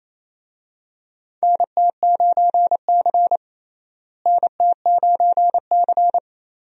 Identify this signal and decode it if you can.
Morse code, heard as a keyed tone, about 28 wpm, 700 Hz. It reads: NT9C NT9C
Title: USA Call Signs - 4 Characters - Repeated 2x - 28wpm